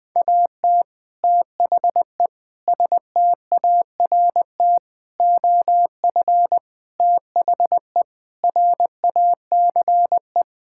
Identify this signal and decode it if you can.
Morse code, heard as a keyed tone, about 20 wpm, 700 Hz. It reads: AT THE START OF THE RACE
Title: Sentences with 5-Word Phrases - 20wpm